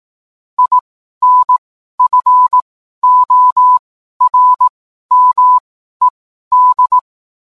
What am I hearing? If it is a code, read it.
Morse code, heard as a keyed tone, about 18 wpm, spaced as for 12 wpm, 1000 Hz. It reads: INFORMED